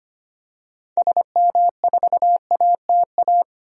Morse code, heard as a keyed tone, about 25 wpm, 700 Hz. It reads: SM4ATA
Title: DXCC-100 Call Signs - 6 Characters - 25wpm